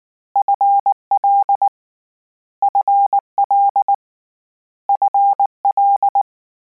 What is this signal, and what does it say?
Morse code, heard as a keyed tone, about 19 wpm, 800 Hz. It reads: FL FL FL
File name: Combinations / Combinations-mp3s / Combinations@18